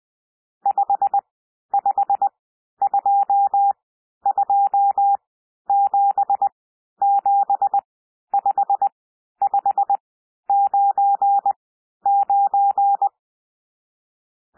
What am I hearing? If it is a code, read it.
Morse code, heard as a keyed tone, about 20 wpm, spaced as for 11 wpm, 800 Hz. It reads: 5522775599
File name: cwmp3 / N1 SOUND NUMBERS 20 wpm